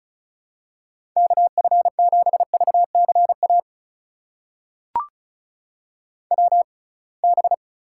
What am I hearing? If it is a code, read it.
Morse code, heard as a keyed tone, about 36 wpm, spaced as for 27 wpm, 700 Hz. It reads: KF7VCA WB
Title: USA Call Signs - 6 Characters - 6x Character Spacing - 35wpm